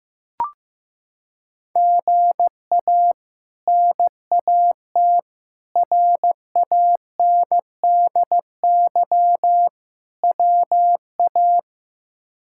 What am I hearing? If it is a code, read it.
Morse code, heard as a keyed tone, about 15 wpm, 700 Hz. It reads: GA NAT RANDY WA
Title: K1USN SST QSO Practice - 15wpm